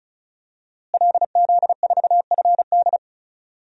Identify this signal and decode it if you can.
Morse code, heard as a keyed tone, about 36 wpm, 700 Hz. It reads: LZ4FD
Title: DXCC-100 Call Signs - 5 Characters - Repeated 3x - 35wpm